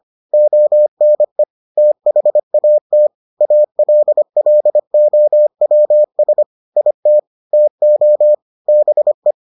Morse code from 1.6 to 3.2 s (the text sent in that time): THAT